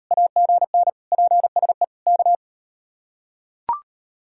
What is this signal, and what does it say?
Morse code, heard as a keyed tone, about 38 wpm, 700 Hz. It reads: AGN PSE K